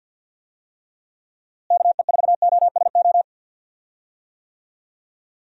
Morse code, heard as a keyed tone, about 53 wpm, 700 Hz. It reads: KE4OSO
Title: USA Call Signs - 6 Characters - Repeated 3x - 50wpm